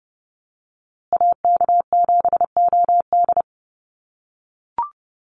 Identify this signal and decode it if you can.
Morse code, heard as a keyed tone, about 30 wpm, 700 Hz. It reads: AK7OD
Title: USA Call Signs Repeated 3x 30wpm